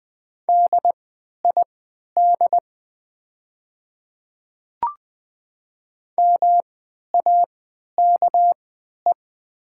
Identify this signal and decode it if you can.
Morse code, heard as a keyed tone, about 20 wpm, spaced as for 11 wpm, 700 Hz. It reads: DID MAKE
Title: 3x Character Spacing - Top 200 Words - 20wpm